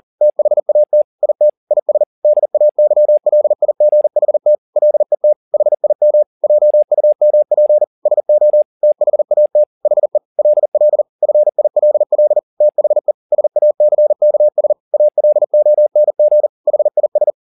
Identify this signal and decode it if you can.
Morse code, heard as a keyed tone, about 40 wpm, 600 Hz. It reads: THAT IT IS DAYLIGHT LET HIM JUMP SO THAT HE LL FILL THE SACKS ALONG HIS